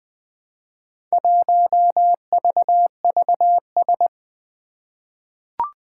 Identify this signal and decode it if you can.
Morse code, heard as a keyed tone, about 20 wpm, 700 Hz. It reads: 1VVS
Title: Four Challenging Letter-Number Combinations 20wpm